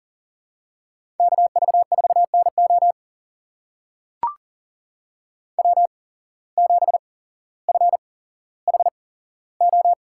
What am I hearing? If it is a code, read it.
Morse code, heard as a keyed tone, about 40 wpm, spaced as for 17 wpm, 700 Hz. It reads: KV4NO W7FHO
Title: USA Call Signs - 5 Characters - 8x Character Spacing - 40wpm